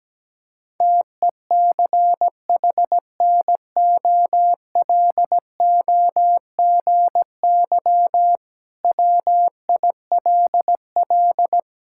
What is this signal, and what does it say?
Morse code, heard as a keyed tone, about 17 wpm, 700 Hz. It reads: TECHNOLOGY WILL
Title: Sentences from Top 1000 Words - Review Part 5 - 17wpm